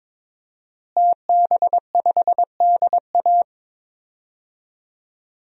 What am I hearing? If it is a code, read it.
Morse code, heard as a keyed tone, about 22 wpm, 700 Hz. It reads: TB5DA
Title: DXCC-100 Call Signs - 5 Characters - Repeated 2x - 22wpm